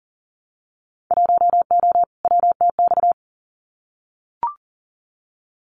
Morse code, heard as a keyed tone, about 40 wpm, 700 Hz. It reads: 1O WTX
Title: Winter Field Day Exchanges Encore 40wpm